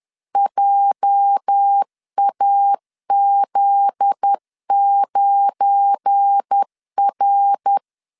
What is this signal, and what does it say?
Morse code, heard as a keyed tone, about 10 wpm, 800 Hz. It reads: JAZ9R
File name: LEZIONI / 8 YJ45 / 8 YJ 10wpm 2651